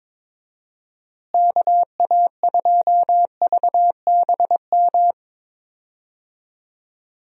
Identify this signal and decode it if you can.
Morse code, heard as a keyed tone, about 22 wpm, 700 Hz. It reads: KA2VBM
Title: USA Call Signs - 6 Characters - Repeated 2x - 22wpm